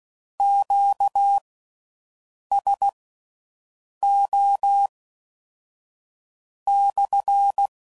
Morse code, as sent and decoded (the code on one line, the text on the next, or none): --.- ... --- -..-.
QSO/